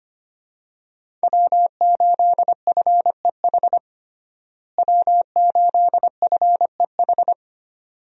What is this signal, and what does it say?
Morse code, heard as a keyed tone, about 25 wpm, 700 Hz. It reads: W8FEH W8FEH
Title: USA Call Signs - 5 Characters - Repeated 2x - 25wpm